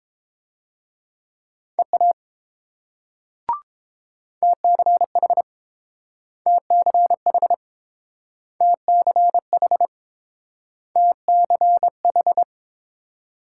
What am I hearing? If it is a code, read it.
Morse code, heard as a keyed tone, about 30 wpm, 700 Hz. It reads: EA TCH TCH TCH TCH